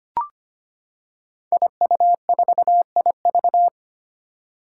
Morse code, heard as a keyed tone, about 25 wpm, 700 Hz. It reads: IU4IV
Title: Five Challenging Letter-Number Combinations 25wpm